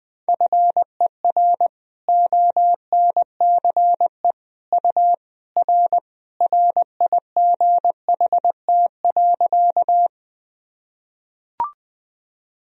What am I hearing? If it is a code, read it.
Morse code, heard as a keyed tone, about 21 wpm, 700 Hz. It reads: FER ONCE U R RIGHT.
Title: Sentences from Top 300 Words and Abbreviations 20wpm